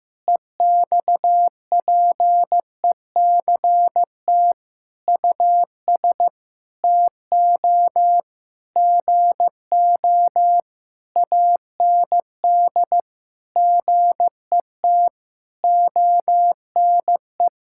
Morse code, heard as a keyed tone, about 16 wpm, 700 Hz. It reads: EXPECT US TO GO AND GET ONE
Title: Sentences from Top 1600 Words 15wpm